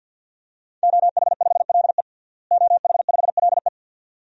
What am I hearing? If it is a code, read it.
Morse code, heard as a keyed tone, about 49 wpm, 700 Hz. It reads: OH5LE OH5LE
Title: DXCC-100 Call Signs - 5 Characters - Repeated 2x - 50wpm